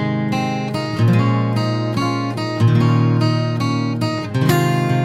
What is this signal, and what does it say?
There is no code.